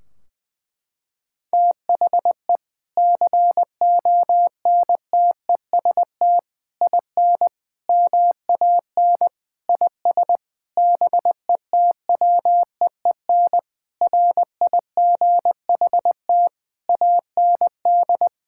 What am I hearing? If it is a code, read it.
Morse code, heard as a keyed tone, about 20 wpm, 700 Hz. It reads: THE CONTEST IN MAN IS BETWEEN RIGHT AND
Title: Sentences from Top 1600 Words 20wpm